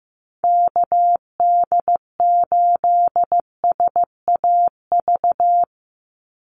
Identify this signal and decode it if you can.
Morse code, heard as a keyed tone, about 15 wpm, 700 Hz. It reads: KD8SAV
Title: Call Signs Repeated 2x - 15wpm